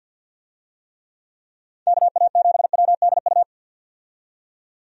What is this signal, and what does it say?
Morse code, heard as a keyed tone, about 52 wpm, 700 Hz. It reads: KA7WDU
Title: USA Call Signs - 6 Characters - 50wpm